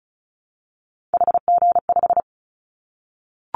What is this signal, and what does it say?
Morse code, heard as a keyed tone, about 36 wpm, 700 Hz. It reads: HG5
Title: Callsign Prefixes for Top 50 DXCC - 35wpm